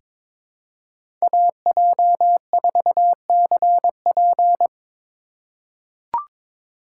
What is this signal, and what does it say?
Morse code, heard as a keyed tone, about 22 wpm, 700 Hz. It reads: AJ4CP
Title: USA Call Signs - 3x Character Spacing - 22wpm